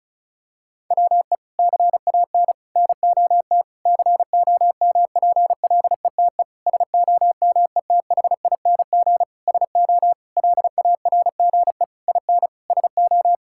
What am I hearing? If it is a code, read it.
Morse code, heard as a keyed tone, about 34 wpm, 700 Hz. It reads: WE CAN NOT COMPLETE SOMETHING SO LARGE IN SO
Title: Sentences from Top 1000 Words - Review Part 6 - 35wpm